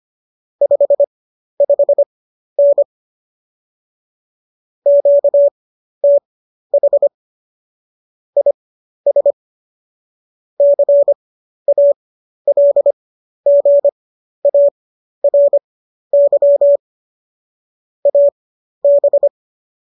Morse code, heard as a keyed tone, about 25 wpm, spaced as for 12 wpm, 570 Hz. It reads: 55N QTH IS CALGARY AB